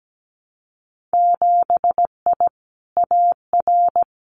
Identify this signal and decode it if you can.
Morse code, heard as a keyed tone, about 17 wpm, 700 Hz. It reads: 7I AR
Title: Winter Field Day Exchanges Encore 17wpm